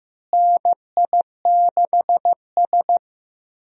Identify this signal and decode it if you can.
Morse code, heard as a keyed tone, about 15 wpm, 700 Hz. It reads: NI6S
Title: USA Call Signs - 4 Characters - Repeated 3x - 15wpm